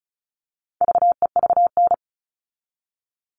Morse code, heard as a keyed tone, about 34 wpm, 700 Hz. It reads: VEVN
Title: All Cut Numbers - 4-Digit 35wpm